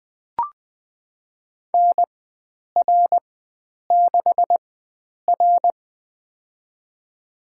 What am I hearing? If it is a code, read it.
Morse code, heard as a keyed tone, about 20 wpm, spaced as for 9 wpm, 700 Hz. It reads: NR6R
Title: USA Call Signs - 4 Characters - 4x Character Spacing - 20wpm